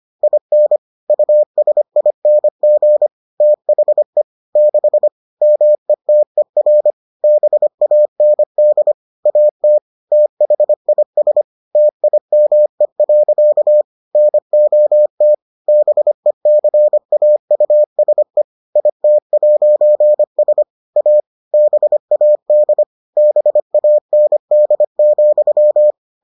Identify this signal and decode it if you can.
Morse code, heard as a keyed tone, about 25 wpm, 600 Hz. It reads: IN USING THE 6 METER BAND AT THIS TIME. NOT BECAUSE IT'S A BAD BAND,